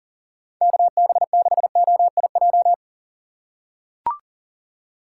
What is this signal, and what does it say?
Morse code, heard as a keyed tone, about 40 wpm, 700 Hz. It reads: KB6OIJ
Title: USA Call Signs - 6 Characters - 3x Character Spacing - 40wpm